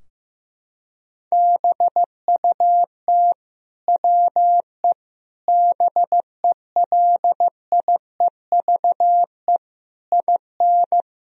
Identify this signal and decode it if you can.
Morse code, heard as a keyed tone, about 15 wpm, 700 Hz. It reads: BUT WE BELIEVE IN
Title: Sentences from Top 1000 Words - Review Part 3 - 15wpm